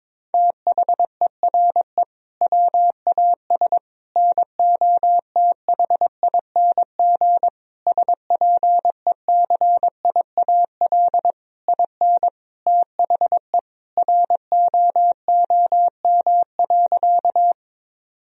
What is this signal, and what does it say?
Morse code, heard as a keyed tone, about 22 wpm, 700 Hz. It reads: THERE WAS NOTHING SPECIAL IN THE ROOM.
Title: Sentences from Top 1000 Words - Review Part 3 - 22wpm